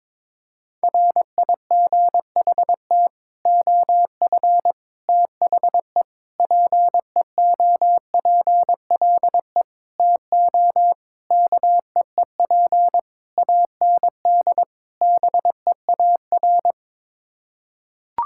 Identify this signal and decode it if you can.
Morse code, heard as a keyed tone, about 22 wpm, 700 Hz. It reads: RIGHT OF THE PEOPLE TO KEEP AND BEAR
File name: practice / 22wpm / Sets of 8 Words - 22wpm